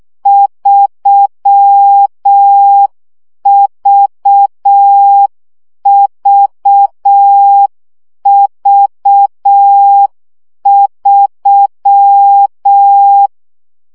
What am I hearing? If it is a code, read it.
Morse code, heard as a keyed tone, about 6 wpm, 800 Hz. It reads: 3VVV3
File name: CW gyakorlas v 3 5 baud